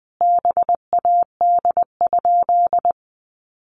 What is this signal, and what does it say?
Morse code, heard as a keyed tone, about 20 wpm, 700 Hz. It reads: BAD?
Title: Sentences from Top 700 Words 20wpm